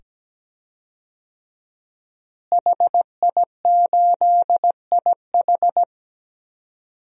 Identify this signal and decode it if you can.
Morse code, heard as a keyed tone, about 17 wpm, 700 Hz. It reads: HI8IH
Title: DXCC-100 Call Signs - 5 Characters - Repeated 2x - 17wpm